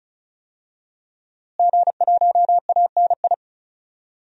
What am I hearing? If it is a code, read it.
Morse code, heard as a keyed tone, about 34 wpm, 700 Hz. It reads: G1ANI
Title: DXCC-100 Call Signs - 5 Characters - Repeated 2x - 35wpm